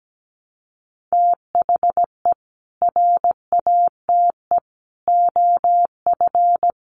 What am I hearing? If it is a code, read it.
Morse code, heard as a keyed tone, about 17 wpm, 700 Hz. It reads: THE RATE OF
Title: Sentences from Top 1100 Words 17wpm